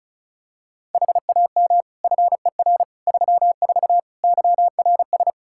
Morse code, heard as a keyed tone, about 36 wpm, 700 Hz. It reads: HAM FER 34 YRS
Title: Practice QSO - Exchange 3 - More Info - 35wpm